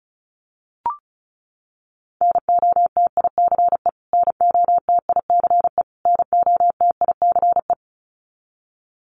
Morse code, heard as a keyed tone, about 34 wpm, 700 Hz. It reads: NOTICE NOTICE NOTICE
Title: Top 500 Words Repeated 35wpm